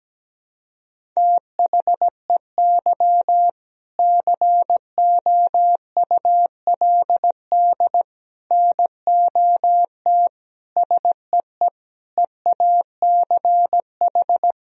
Morse code, heard as a keyed tone, about 17 wpm, 700 Hz. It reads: THEY COULD NOT SEE EACH
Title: Sentences from Top 1000 Words - Review Part 1 - 17wpm